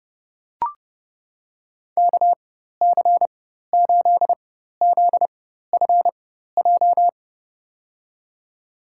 Morse code, heard as a keyed tone, about 30 wpm, spaced as for 14 wpm, 700 Hz. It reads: KC8ZFJ
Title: USA Call Signs - 4x Character Spacing - 30wpm